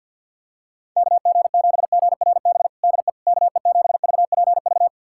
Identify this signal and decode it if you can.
Morse code, heard as a keyed tone, about 47 wpm, 700 Hz. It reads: KG7GRB DE KE7VPV
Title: USA Callsign DE Callsign - 6 Characters - 50wpm